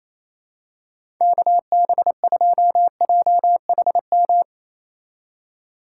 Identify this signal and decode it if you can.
Morse code, heard as a keyed tone, about 28 wpm, 700 Hz. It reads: KB2JHM
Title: USA Call Signs - 6 Characters - 4x Character Spacing - 28wpm